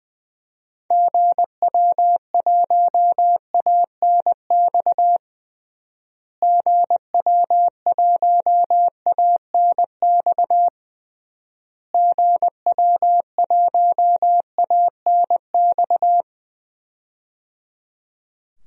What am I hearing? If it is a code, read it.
Morse code, heard as a keyed tone, about 20 wpm, 700 Hz. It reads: GW1ANX GW1ANX GW1ANX
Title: DXCC-100 Call Signs - 6 Characters - Repeated 3x - 20wpm